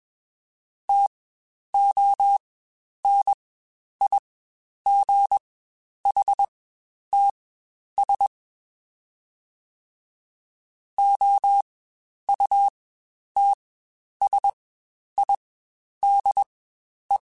Morse code, heard as a keyed tone, about 21 wpm, spaced as for 10 wpm, 800 Hz. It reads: TONIGHTS OUTSIDE
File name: Spaced Characters / Spaced Characters-mp3s / Weather5 1 sp @20